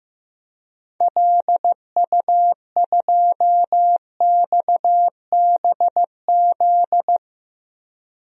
Morse code, heard as a keyed tone, about 15 wpm, 700 Hz. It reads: LU2XBZ
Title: DXCC-100 Call Signs - 6 Characters - 15wpm